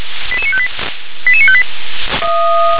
Five-tone selective call (ZVEI), heard as quick runs of five tones, tones of about 70 ms. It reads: 88958 88958